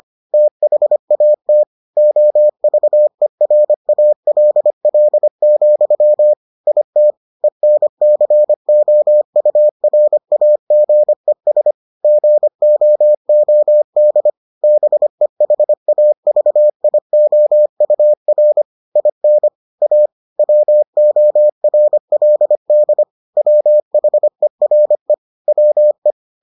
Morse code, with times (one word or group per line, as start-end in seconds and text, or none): 0.3-1.6 THAT
2.0-6.3 OVERALL,
6.7-7.1 IT
7.4-11.7 ENCOURAGES
12.0-14.3 GOOD
14.6-18.6 BEHAVIOUR
19.0-19.5 IN
19.8-20.1 A
20.4-23.0 WORLD
23.4-25.1 WHERE
25.5-26.1 WE